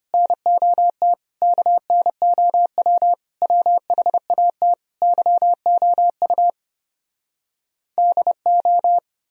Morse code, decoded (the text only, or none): NOT KNOW WHAT YOU DO